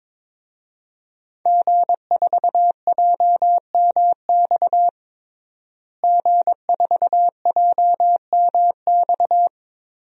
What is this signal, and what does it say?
Morse code, heard as a keyed tone, about 22 wpm, 700 Hz. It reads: G4JMX G4JMX
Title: DXCC-100 Call Signs - 5 Characters - Repeated 3x - 22wpm